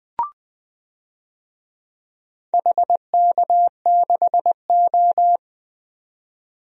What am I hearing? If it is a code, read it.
Morse code, heard as a keyed tone, about 20 wpm, 700 Hz. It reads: HK6O
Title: DXCC-100 Call Signs - 4 Characters - 20wpm